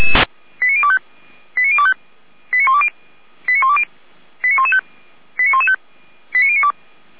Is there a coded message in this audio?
Yes: five-tone selective call (ZVEI) from 0.6 to 6.7 s, heard as quick runs of five tones, tones of about 70 ms. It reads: 89025 89025 89120 89120 89115 89115 89092